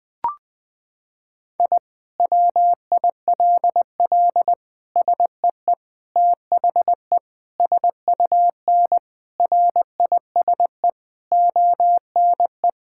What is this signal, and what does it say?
Morse code, heard as a keyed tone, about 20 wpm, 700 Hz. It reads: I WILL SEE THE SUN RISE ONE
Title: Sentences from Top 1000 Words - Review Part 6 - 20wpm